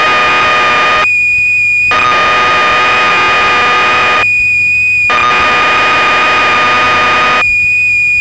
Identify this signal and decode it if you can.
Cassette data (Kansas City type).